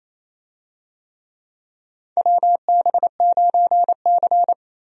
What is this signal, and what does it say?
Morse code, heard as a keyed tone, about 28 wpm, 700 Hz. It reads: WB9C
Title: USA Call Signs - 4 Characters - 28wpm